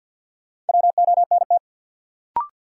Morse code, heard as a keyed tone, about 52 wpm, 700 Hz. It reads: WONT
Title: Sentences from Top 1000 Words - Review Part 6 - New Word Drill - 50wpm